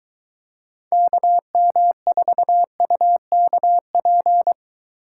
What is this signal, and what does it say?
Morse code, heard as a keyed tone, about 23 wpm, 700 Hz. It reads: KM4UKP